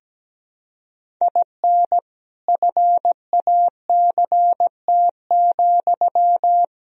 Morse code, heard as a keyed tone, about 17 wpm, 700 Hz. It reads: IN FACT,
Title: Sentences from Top 1000 Words - Review Part 3 - 17wpm